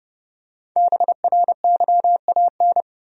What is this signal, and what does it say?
Morse code, heard as a keyed tone, about 30 wpm, 700 Hz. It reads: BRYAN
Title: USA Common Names - Max 5 Letters - 30wpm